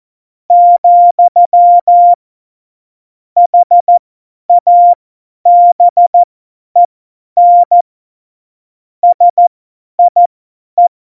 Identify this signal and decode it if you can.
Morse code, heard as a keyed tone, about 14 wpm, spaced as for 10 wpm, 700 Hz. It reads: , HABEN SIE
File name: Morsekurs / Witze / Witz 29. Januar 2019, 50 F70